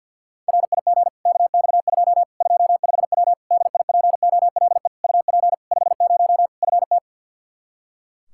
Morse code, heard as a keyed tone, about 49 wpm, 700 Hz. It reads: RIG KX2 25W DIPOLE UP 50 FT
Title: K1USN SST QSO Practice 50wpm